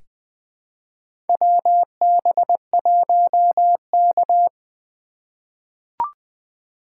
Morse code, heard as a keyed tone, about 20 wpm, 700 Hz. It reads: WB1K
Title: USA Call Signs - 4 Characters - 3x Character Spacing - 20wpm